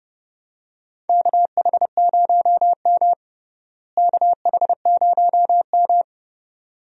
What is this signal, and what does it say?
Morse code, heard as a keyed tone, about 31 wpm, 700 Hz. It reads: KH0M KH0M
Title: USA Call Signs - 4 Characters - Repeated 2x - 30wpm